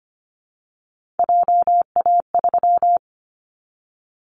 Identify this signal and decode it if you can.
Morse code, heard as a keyed tone, about 25 wpm, 700 Hz. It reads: JA3
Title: Callsign Prefixes for Top 25 DXCC - 25wpm